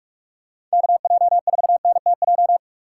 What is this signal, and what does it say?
Morse code, heard as a keyed tone, about 42 wpm, 700 Hz. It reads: KJ4NTJ
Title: USA Call Signs - 6 Characters - 4x Character Spacing - 45wpm